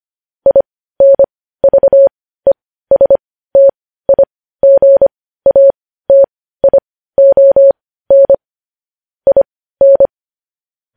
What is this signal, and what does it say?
Morse code, heard as a keyed tone, about 25 wpm, spaced as for 15 wpm, 550 Hz. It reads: INVESTIGATION IN
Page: news headlines in morse code at 15 wpm